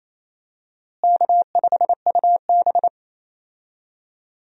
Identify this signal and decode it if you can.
Morse code, heard as a keyed tone, about 27 wpm, 700 Hz. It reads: K5UB
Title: USA Call Signs - 4 Characters - Repeated 2x - 28wpm